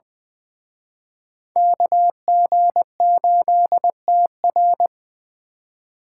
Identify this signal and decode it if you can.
Morse code, heard as a keyed tone, about 24 wpm, 700 Hz. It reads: KG8TR
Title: Speed Racing - USA 2x2 Call Signs 20wpm